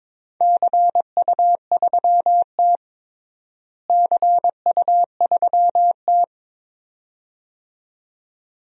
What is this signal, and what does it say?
Morse code, heard as a keyed tone, about 22 wpm, 700 Hz. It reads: CU3T CU3T
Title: DXCC-100 Call Signs - 4 Characters - Repeated 2x - 22wpm